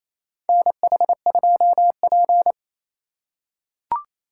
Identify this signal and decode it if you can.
Morse code, heard as a keyed tone, about 28 wpm, 700 Hz. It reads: NH2P
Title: USA Call Signs - 4 Characters - 4x Character Spacing - 28wpm